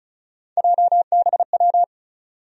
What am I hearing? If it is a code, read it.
Morse code, heard as a keyed tone, about 36 wpm, 700 Hz. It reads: JBW
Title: USA Call Sign Postfixes - 3 Letters 35wpm